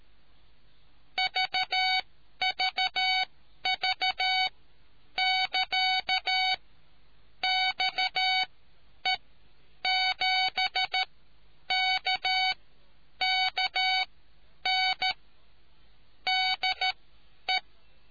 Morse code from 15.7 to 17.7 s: DE